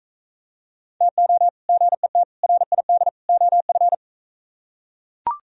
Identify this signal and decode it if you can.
Morse code, heard as a keyed tone, about 43 wpm, 700 Hz. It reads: TO GET RID OF